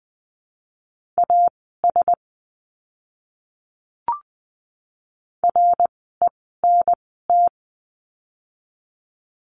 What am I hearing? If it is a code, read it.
Morse code, heard as a keyed tone, about 20 wpm, 700 Hz. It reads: A S R E N T
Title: 054 Lesson 10f - Letter r - Words 20wpm